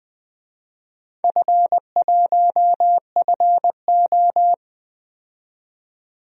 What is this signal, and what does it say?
Morse code, heard as a keyed tone, about 20 wpm, 700 Hz. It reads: F1FO
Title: DXCC-100 Call Signs - 4 Characters - 20wpm